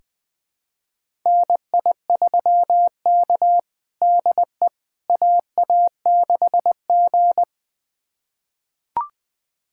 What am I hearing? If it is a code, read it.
Morse code, heard as a keyed tone, about 20 wpm, 700 Hz. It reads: NI3K DE AA6G
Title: USA Callsign DE Callsign - 4 Characters - 20wpm